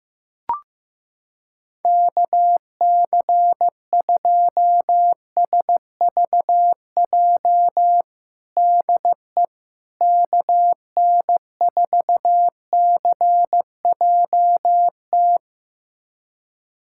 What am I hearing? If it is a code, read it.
Morse code, heard as a keyed tone, about 15 wpm, 700 Hz. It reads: KC2SVJ DE KN4CJT
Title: USA Callsign DE Callsign - 6 Characters - 15wpm